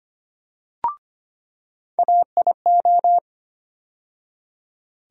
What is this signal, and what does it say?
Morse code, heard as a keyed tone, about 25 wpm, 700 Hz. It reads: AIO
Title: USA Call Sign Postfixes - 3 Letters 25wpm